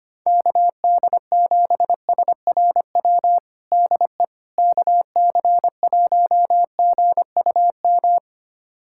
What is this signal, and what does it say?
Morse code, heard as a keyed tone, about 25 wpm, 700 Hz. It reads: KD7SRW DE KC1GUM
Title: USA Callsign DE Callsign - 6 Characters - 25wpm